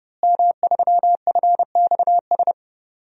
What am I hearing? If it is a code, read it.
Morse code, heard as a keyed tone, about 30 wpm, 700 Hz. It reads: M3FXS
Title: DXCC-100 Call Signs - 5 Characters - Repeated 2x - 30wpm